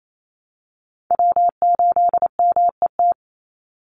Morse code, heard as a keyed tone, about 28 wpm, 700 Hz. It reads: W8MET
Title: Call Signs Repeated 2x - 28wpm